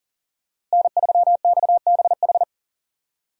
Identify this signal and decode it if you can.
Morse code, heard as a keyed tone, about 40 wpm, 700 Hz. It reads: N3XBH
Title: USA Call Signs - 5 Characters - Repeated 2x - 40wpm